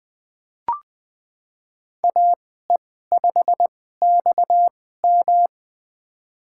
Morse code, heard as a keyed tone, about 20 wpm, spaced as for 14 wpm, 700 Hz. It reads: AE5XM
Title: USA Call Signs - 5 Characters - 2x Character Spacing - 20wpm